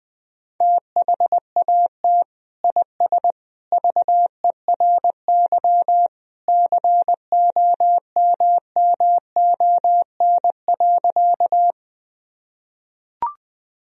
Morse code, heard as a keyed tone, about 20 wpm, 700 Hz. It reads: THAT IS VERY COMMON.
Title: Sentences from Top 1000 Words - Review Part 3 - 20wpm